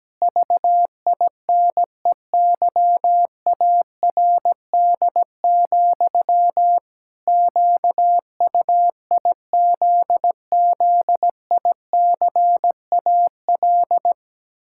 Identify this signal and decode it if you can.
Morse code, heard as a keyed tone, about 17 wpm, 700 Hz. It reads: VINEYARD, QUIZZICAL
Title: Pangram Challenge v2 - 17wpm